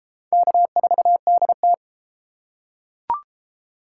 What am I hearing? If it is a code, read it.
Morse code, heard as a keyed tone, about 29 wpm, 700 Hz. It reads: K4DT